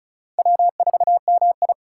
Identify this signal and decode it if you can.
Morse code, heard as a keyed tone, about 34 wpm, 700 Hz. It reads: W4MI
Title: USA Call Signs - 4 Characters - 4x Character Spacing - 35wpm